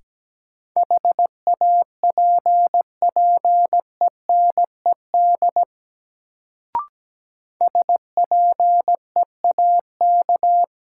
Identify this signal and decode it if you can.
Morse code, heard as a keyed tone, about 17 wpm, 700 Hz. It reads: HAPPENED SPEAK
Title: Sentences from Top 1000 Words - Review Part 2 - New Word Drill - 17wpm